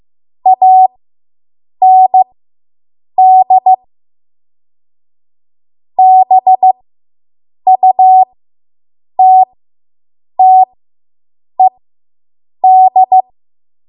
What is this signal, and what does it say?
Morse code, heard as a keyed tone, about 15 wpm, spaced as for 7 wpm, 750 Hz. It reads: AND BUTTED